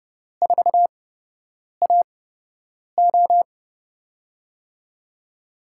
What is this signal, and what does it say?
Morse code, heard as a keyed tone, about 31 wpm, spaced as for 8 wpm, 700 Hz. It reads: 4AO
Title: USA Call Signs - 5 Characters - 8x Character Spacing - 30wpm